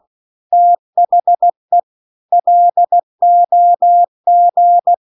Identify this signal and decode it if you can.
Morse code, heard as a keyed tone, about 16 wpm, 700 Hz. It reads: THE LOG
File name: CW-The-Big-Twoheaded-River-2-16WPM